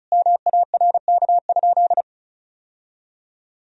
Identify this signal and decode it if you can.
Morse code, heard as a keyed tone, about 34 wpm, 700 Hz. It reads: MARK?
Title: Sentences from Top 1000 Words - Review Part 1 - 35wpm